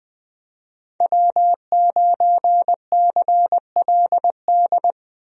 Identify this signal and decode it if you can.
Morse code, heard as a keyed tone, about 20 wpm, 700 Hz. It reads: W9CLD
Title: USA Call Signs - 5 Characters - Repeated 2x - 20wpm